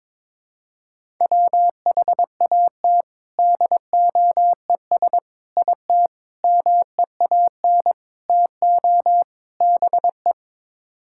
Morse code, heard as a keyed tone, about 21 wpm, 700 Hz. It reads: WHAT DOES IT MEAN TO BE
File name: practice / 22wpm / Sets of 6 Words - 22wpm